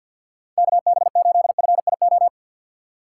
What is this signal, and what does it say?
Morse code, heard as a keyed tone, about 53 wpm, 700 Hz. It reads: KB8FIO
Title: USA Call Signs - 6 Characters - 3x Character Spacing - 50wpm